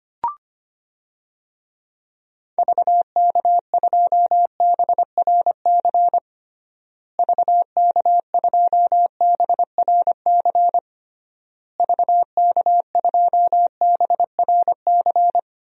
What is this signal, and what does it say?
Morse code, heard as a keyed tone, about 25 wpm, 700 Hz. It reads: VK2BRC VK2BRC VK2BRC
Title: DXCC-100 Call Signs - 6 Characters - Repeated 3x - 25wpm